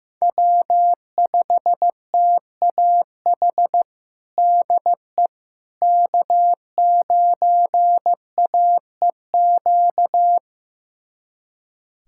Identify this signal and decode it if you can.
Morse code, heard as a keyed tone, about 15 wpm, 700 Hz. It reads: W5TAH DE K9AEQ
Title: USA Callsign DE Callsign - 5 Characters - 15wpm